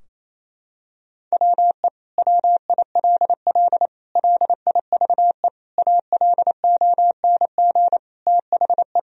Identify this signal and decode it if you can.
Morse code, heard as a keyed tone, about 28 wpm, 700 Hz. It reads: WE WILL LIVE ALONG THE
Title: Sentences from Top 1000 Words - Review Part 4 - 28wpm